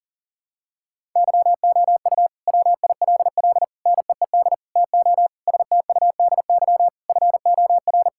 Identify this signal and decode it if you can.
Morse code, heard as a keyed tone, about 40 wpm, 700 Hz. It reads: YOU WILL NEED TO STUDY FOR
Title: Sentences from Top 1000 Words - Review Part 5 - 40wpm